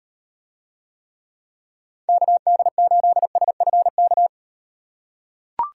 Morse code, heard as a keyed tone, about 38 wpm, 700 Hz. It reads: KD8SFK